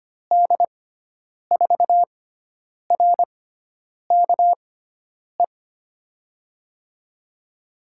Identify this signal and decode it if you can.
Morse code, heard as a keyed tone, about 25 wpm, spaced as for 9 wpm, 700 Hz. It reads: D4RKE
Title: USA Call Signs - 6 Characters - 6x Character Spacing - 25wpm